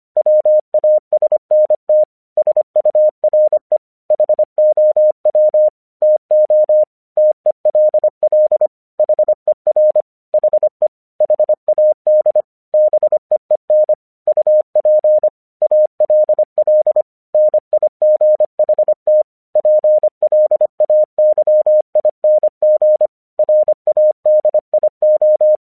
Morse code, heard as a keyed tone, about 25 wpm, 610 Hz. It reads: WASNT SURE HOW TO TELL HER HE HAD BEEN UP ALL NIGHT PLAYING RADIO
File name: 1 English 25wpm / AQC35